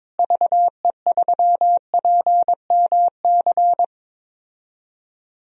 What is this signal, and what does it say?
Morse code, heard as a keyed tone, about 22 wpm, 700 Hz. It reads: VE3PMC
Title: DXCC-100 Call Signs - 6 Characters - Repeated 3x - 22wpm